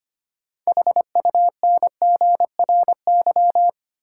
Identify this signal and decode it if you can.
Morse code, heard as a keyed tone, about 25 wpm, 700 Hz. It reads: HUNGRY
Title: Sentences from Top 1400 Words - New Word Drill Alt - 25wpm